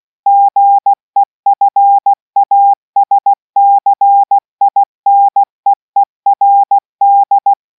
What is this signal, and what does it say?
Morse code, heard as a keyed tone, about 16 wpm, 800 Hz. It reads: GEFASCINEERD